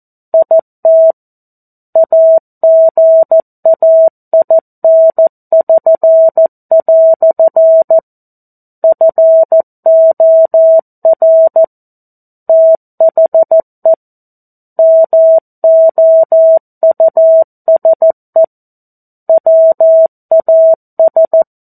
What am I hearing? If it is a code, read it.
Morse code, heard as a keyed tone, about 14 wpm, 650 Hz. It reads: IT AGAIN<SN>" FOR THE MOUSE WAS
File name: 02-Chapter-02 14 WPM